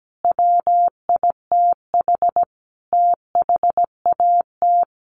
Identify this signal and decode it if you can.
Morse code, heard as a keyed tone, about 17 wpm, 700 Hz. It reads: WITH THAT
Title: Sentences from Top 300 Words 17wpm